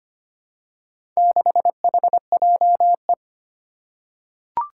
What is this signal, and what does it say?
Morse code, heard as a keyed tone, about 25 wpm, 700 Hz. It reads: 6HJE
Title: Four Challenging Letter-Number Combinations 25wpm